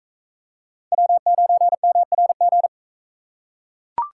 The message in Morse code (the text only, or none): W9MRG